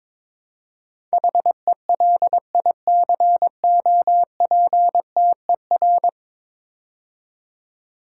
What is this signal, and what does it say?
Morse code, heard as a keyed tone, about 22 wpm, 700 Hz. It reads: HELICOPTER
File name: practice / 22wpm / 10 Letter Words - First 3 Letters Given - 22wpm